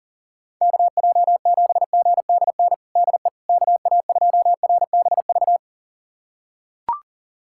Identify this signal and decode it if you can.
Morse code, heard as a keyed tone, about 40 wpm, 700 Hz. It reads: KJ7GDN DE KA2RBV
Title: USA Callsign DE Callsign - 6 Characters - 40wpm